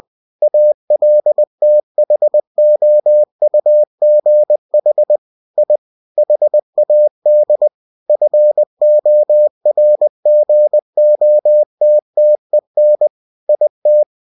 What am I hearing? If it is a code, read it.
Morse code, heard as a keyed tone, about 20 wpm, 600 Hz. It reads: ALTHOUGH I HAD FORGOTTEN IT
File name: cw-books / winnie-the-pooh / winnie-20wpm-page-0034